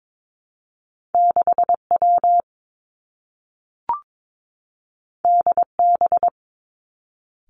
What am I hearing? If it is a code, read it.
Morse code, heard as a keyed tone, about 22 wpm, 700 Hz. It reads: 6 W D B
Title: Two Challenging Letter-Number Combinations 22wpm